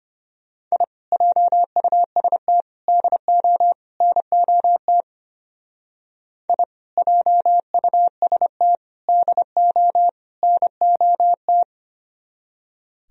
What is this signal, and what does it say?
Morse code, heard as a keyed tone, about 31 wpm, 700 Hz. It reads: I JUST DO NOT I JUST DO NOT
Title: Speed-Racing - Sets of 4 Words 25wpm